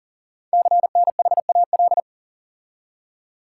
Morse code, heard as a keyed tone, about 39 wpm, 700 Hz. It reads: CNHAL
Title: Code Groups - 5 Letters - 40wpm